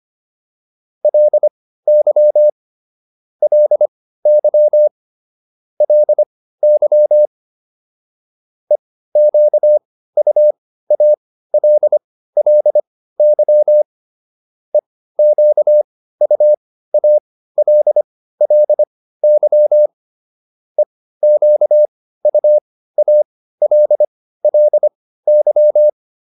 Morse code, heard as a keyed tone, about 25 wpm, spaced as for 15 wpm, 600 Hz. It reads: LY LY LY EQUALLY EQUALLY EQUALLY